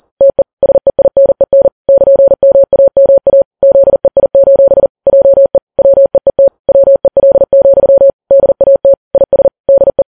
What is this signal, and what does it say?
Morse code, heard as a keyed tone, about 40 wpm, 550 Hz. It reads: TE SEINEN <KN>MAMA ZEI: JE WEET WEL, DAT IS DE
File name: Files Web tls / Morse on9fv / morse40kps